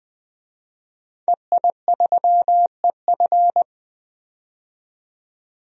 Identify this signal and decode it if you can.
Morse code, heard as a keyed tone, about 20 wpm, 700 Hz. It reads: EI3EF
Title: DXCC-100 Call Signs - 5 Characters - 20wpm